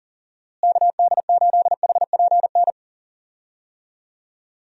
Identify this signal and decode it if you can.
Morse code, heard as a keyed tone, about 40 wpm, 700 Hz. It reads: KD8HPN
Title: USA Call Signs - 6 Characters - Repeated 3x - 40wpm